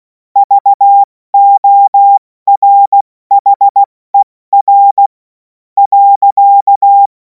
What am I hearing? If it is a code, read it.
Morse code, heard as a keyed tone, about 16 wpm, 800 Hz. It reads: VORHER .
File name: Morsekurs / Witze / Witz 11. August 2026 70 F80